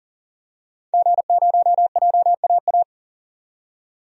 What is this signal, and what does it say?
Morse code, heard as a keyed tone, about 40 wpm, 700 Hz. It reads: G0JAA
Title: DXCC-100 Call Signs - 5 Characters - 40wpm